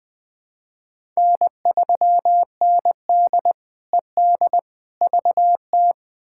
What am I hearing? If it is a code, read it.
Morse code, heard as a keyed tone, about 21 wpm, 700 Hz. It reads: N3ND ED VT
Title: K1USN SST QSO Practice - 20wpm